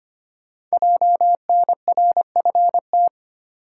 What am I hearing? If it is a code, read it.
Morse code, heard as a keyed tone, about 25 wpm, 700 Hz. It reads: JNRFT
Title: Code Groups - 5 Letter:Numbers - 25wpm